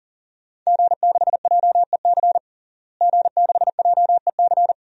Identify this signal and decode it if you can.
Morse code, heard as a keyed tone, about 40 wpm, 700 Hz. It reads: G6JEC G6JEC
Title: DXCC-100 Call Signs - 5 Characters - Repeated 2x - 40wpm